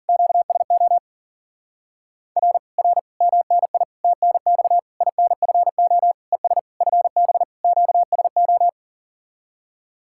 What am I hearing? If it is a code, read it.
Morse code, heard as a keyed tone, about 50 wpm, 700 Hz. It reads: QSO R R MNI TNX INFO ES FB QSO
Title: Speed-Racing Common QSO Phrases - 40wpm